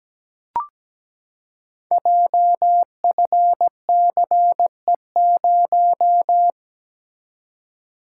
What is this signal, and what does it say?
Morse code, heard as a keyed tone, about 17 wpm, 700 Hz. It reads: JFCE0
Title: Code Groups - 5 Letter:Numbers - 17wpm